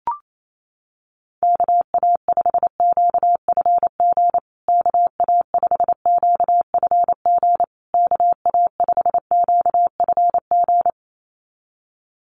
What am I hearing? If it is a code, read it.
Morse code, heard as a keyed tone, about 28 wpm, 700 Hz. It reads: KA5QFG KA5QFG KA5QFG
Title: USA Call Signs Repeated 3x 28wpm